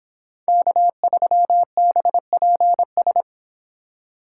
Morse code, heard as a keyed tone, about 29 wpm, 700 Hz. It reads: K3BPS